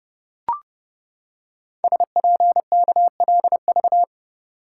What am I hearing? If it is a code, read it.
Morse code, heard as a keyed tone, about 30 wpm, 700 Hz. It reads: SPKLV
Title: Code Groups - 5 Letters - 30wpm